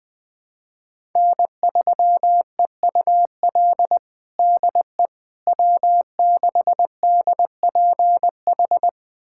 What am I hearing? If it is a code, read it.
Morse code, heard as a keyed tone, about 21 wpm, 700 Hz. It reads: N3EUL DE W6DPH
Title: USA Callsign DE Callsign - 5 Characters - 20wpm